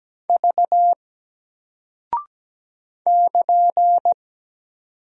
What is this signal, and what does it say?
Morse code, heard as a keyed tone, about 17 wpm, 700 Hz. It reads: V <KN>
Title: Single Character (Letter:Number:Punctuation:Prosign) 17wpm